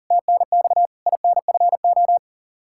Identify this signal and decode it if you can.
Morse code, heard as a keyed tone, about 37 wpm, 700 Hz. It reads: TNX INFO
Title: Practice QSO - Exchange 3 - More Info - 40wpm